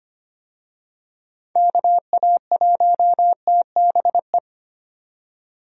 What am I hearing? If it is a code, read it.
Morse code, heard as a keyed tone, about 25 wpm, 700 Hz. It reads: KA1TBE
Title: USA Call Signs - 6 Characters - 25wpm